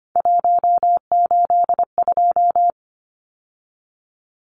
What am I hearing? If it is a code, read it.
Morse code, heard as a keyed tone, about 25 wpm, 690 Hz. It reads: 182
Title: Numbers - 3-Digit 25wpm